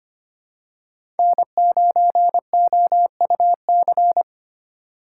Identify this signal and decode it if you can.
Morse code, heard as a keyed tone, about 25 wpm, 700 Hz. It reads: N9OUC
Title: USA Call Signs - 5 Characters - 4x Character Spacing - 25wpm